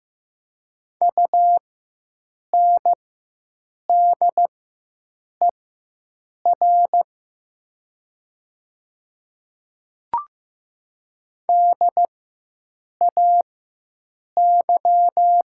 Morse code, heard as a keyed tone, about 15 wpm, spaced as for 7 wpm, 690 Hz. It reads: UNDER DAY
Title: 4x Character Spacing - Top 500 Words - 15wpm